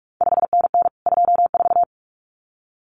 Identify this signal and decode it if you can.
Morse code, heard as a keyed tone, about 46 wpm, 700 Hz. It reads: 5NN 24
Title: CQ WW DX Contest Exchange Encore - 45wpm